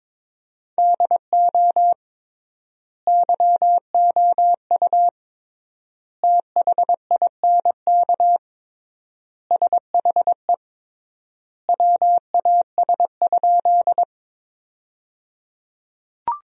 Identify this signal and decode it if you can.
Morse code, heard as a keyed tone, about 22 wpm, 700 Hz. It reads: DO YOU THINK SHE WAS?
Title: 3x Word-Spacing - Sentences from Top 200 Words - 22wpm